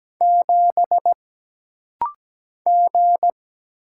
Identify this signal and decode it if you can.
Morse code, heard as a keyed tone, about 17 wpm, 700 Hz. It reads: 7 G
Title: Single Letter-Number - Rapid-Fire 17wpm